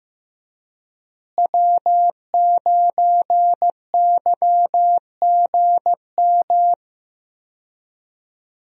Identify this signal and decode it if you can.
Morse code, heard as a keyed tone, about 15 wpm, 700 Hz. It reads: W9YGM
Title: USA Call Signs - 5 Characters - Repeated 3x - 15wpm